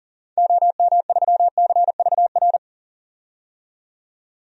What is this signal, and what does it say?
Morse code, heard as a keyed tone, about 40 wpm, 700 Hz. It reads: OM3CVR
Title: DXCC-100 Call Signs - 6 Characters - 40wpm